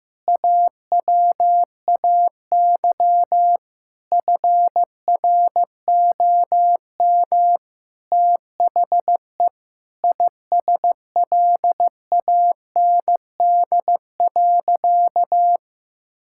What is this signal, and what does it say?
Morse code, heard as a keyed tone, about 15 wpm, 700 Hz. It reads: AWAY FROM THE ISLAND.
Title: Sentences from Top 1000 Words - Review Part 5 - 15wpm